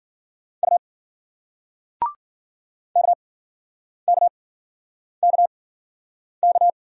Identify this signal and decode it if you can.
Morse code, heard as a keyed tone, about 51 wpm, spaced as for 9 wpm, 700 Hz. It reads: U KKKK